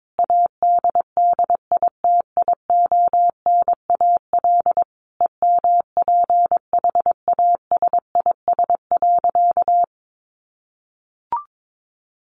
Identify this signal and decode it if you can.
Morse code, heard as a keyed tone, about 22 wpm, 700 Hz. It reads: ADDITIONAL EMPHASIS.
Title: Sentences from Top 1400 Words 22wpm